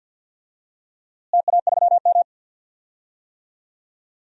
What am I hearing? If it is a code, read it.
Morse code, heard as a keyed tone, about 50 wpm, 700 Hz. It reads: TA3M
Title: DXCC-100 Call Signs - 4 Characters - Repeated 3x - 50wpm